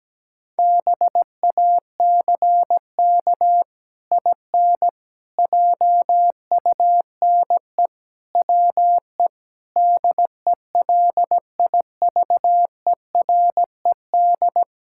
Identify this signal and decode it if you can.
Morse code, heard as a keyed tone, about 17 wpm, 700 Hz. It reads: BACK IN JUNE WE DELIVERED
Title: Pangram Challenge v2 - 17wpm